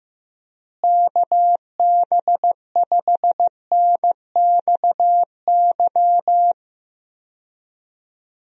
Morse code, heard as a keyed tone, about 15 wpm, 700 Hz. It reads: KB5NXY
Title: USA Call Signs - 6 Characters - Repeated 3x - 15wpm